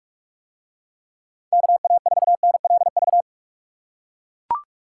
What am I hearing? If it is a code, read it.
Morse code, heard as a keyed tone, about 43 wpm, 700 Hz. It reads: KA4NLV